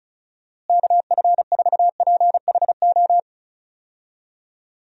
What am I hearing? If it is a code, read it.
Morse code, heard as a keyed tone, about 36 wpm, 700 Hz. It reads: KF4PHO
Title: USA Call Signs - 6 Characters - Repeated 3x - 35wpm